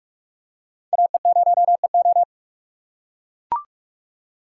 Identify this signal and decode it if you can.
Morse code, heard as a keyed tone, about 45 wpm, 700 Hz. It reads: AE0EO
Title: USA Call Signs - 5 Characters - 4x Character Spacing - 45wpm